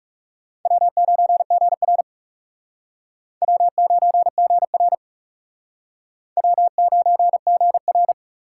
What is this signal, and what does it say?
Morse code, heard as a keyed tone, about 36 wpm, 700 Hz. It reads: W9GR W9GR W9GR
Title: Speed Racing - USA 1x2 Call Signs 30wpm